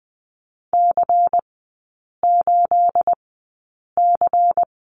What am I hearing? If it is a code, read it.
Morse code, heard as a keyed tone, about 20 wpm, spaced as for 8 wpm, 700 Hz. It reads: C8C
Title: 218 Lesson 35e - Number 0 - Three Characters 20wpm